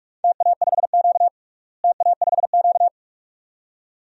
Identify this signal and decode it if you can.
Morse code, heard as a keyed tone, about 46 wpm, 700 Hz. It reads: TA5Q TA5Q
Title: DXCC-100 Call Signs - 4 Characters - Repeated 2x - 45wpm